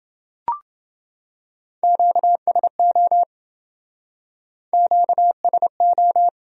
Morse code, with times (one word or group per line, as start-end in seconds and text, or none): 1.8-3.2 QSO
4.7-6.3 QSO